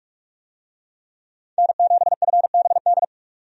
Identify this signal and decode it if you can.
Morse code, heard as a keyed tone, about 42 wpm, 700 Hz. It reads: N7FBD
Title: USA Call Signs - 5 Characters - 45wpm